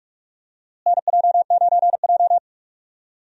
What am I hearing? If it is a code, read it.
Morse code, heard as a keyed tone, about 46 wpm, 700 Hz. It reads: NJ9J
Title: USA Call Signs - 4 Characters - Repeated 2x - 45wpm